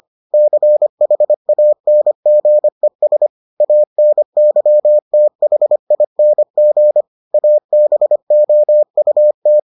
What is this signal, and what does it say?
Morse code, heard as a keyed tone, about 25 wpm, 600 Hz. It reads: CHANGES ANYTHING ABOUT